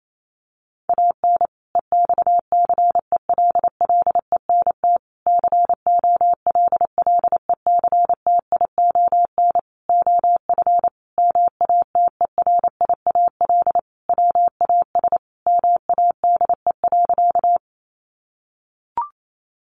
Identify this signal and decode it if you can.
Morse code, heard as a keyed tone, about 28 wpm, 700 Hz. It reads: AN EXCELLENT COLLECTION OF MATERIAL WAS MADE.
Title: Sentences from Top 1200 Words 28wpm